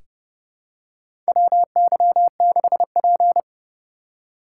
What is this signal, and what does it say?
Morse code, heard as a keyed tone, about 31 wpm, 700 Hz. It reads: WY6P
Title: USA Call Signs - 4 Characters - 3x Character Spacing - 30wpm